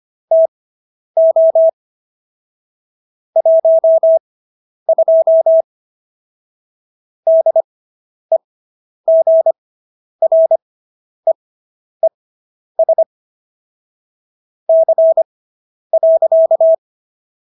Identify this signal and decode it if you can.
Morse code, heard as a keyed tone, about 25 wpm, spaced as for 10 wpm, 650 Hz. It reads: TO 12 DEGREES C.